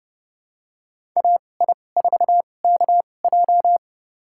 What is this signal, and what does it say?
Morse code, heard as a keyed tone, about 30 wpm, spaced as for 22 wpm, 700 Hz. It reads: AI4KJ
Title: USA Call Signs - 2x Character Spacing - 30wpm